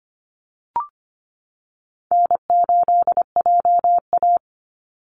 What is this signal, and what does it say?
Morse code, heard as a keyed tone, about 25 wpm, 700 Hz. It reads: N8JA
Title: Easy Call Signs - 25wpm